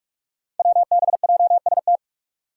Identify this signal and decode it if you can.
Morse code, heard as a keyed tone, about 46 wpm, 700 Hz. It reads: WBJST
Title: Five Challenging Letter-Number Combinations 45wpm